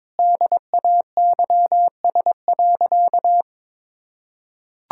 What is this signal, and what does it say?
Morse code, heard as a keyed tone, about 21 wpm, 700 Hz. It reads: DAYS.
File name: practice / 22wpm / Sentences with 3-Word Phrases - 22wpm